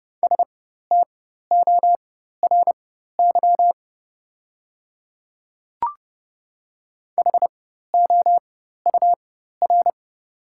Morse code, heard as a keyed tone, about 31 wpm, spaced as for 14 wpm, 690 Hz. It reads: STORY HOUR
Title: 4x Character Spacing - Top 500 Words - 30wpm